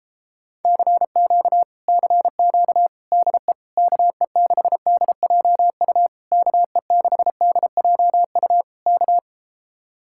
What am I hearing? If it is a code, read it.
Morse code, heard as a keyed tone, about 38 wpm, 700 Hz. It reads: CQ CQ DE KE6DJU KE6DJU K